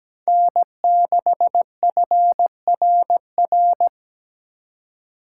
Morse code, heard as a keyed tone, about 20 wpm, 700 Hz. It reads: N6FRR
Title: Speed Racing - USA 1x3 Call Signs 17wpm